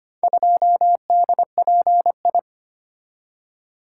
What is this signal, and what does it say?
Morse code, heard as a keyed tone, about 25 wpm, 700 Hz. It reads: 2DPI
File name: practice / 25wpm / Four Letter-Number Combinations - 25wpm